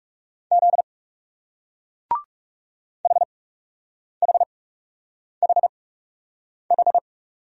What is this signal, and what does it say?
Morse code, heard as a keyed tone, about 40 wpm, spaced as for 8 wpm, 700 Hz. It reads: Z HHHH